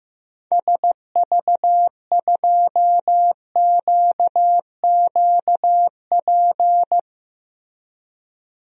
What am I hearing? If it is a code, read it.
Morse code, heard as a keyed tone, about 15 wpm, 700 Hz. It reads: SV2QQP
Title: DXCC-100 Call Signs - 6 Characters - 15wpm